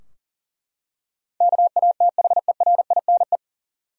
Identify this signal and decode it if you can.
Morse code, heard as a keyed tone, about 40 wpm, 700 Hz. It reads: KATHERINE
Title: USA Common Names - 40wpm